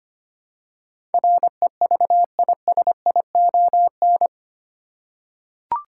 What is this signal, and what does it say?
Morse code, heard as a keyed tone, about 25 wpm, 700 Hz. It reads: REVISION
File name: Eight Letter Words - 25wpm